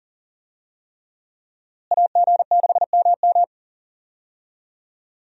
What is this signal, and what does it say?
Morse code, heard as a keyed tone, about 39 wpm, 700 Hz. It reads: AG6MM
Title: USA Call Signs - 5 Characters - Repeated 3x - 40wpm